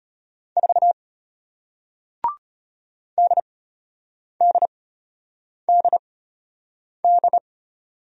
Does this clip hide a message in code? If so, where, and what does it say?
Yes: Morse code from 0.6 to 7.4 s, heard as a keyed tone, about 33 wpm, spaced as for 8 wpm, 700 Hz. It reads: 4 DDDD